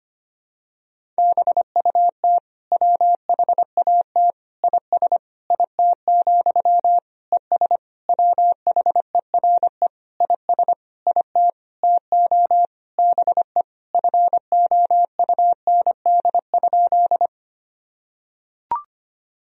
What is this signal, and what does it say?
Morse code, heard as a keyed tone, about 25 wpm, 700 Hz. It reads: BUT WHAT IS IT, ES WHERE IS IT TO BE FOUND?
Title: Sentences from Top 300 Words and Abbreviations 25wpm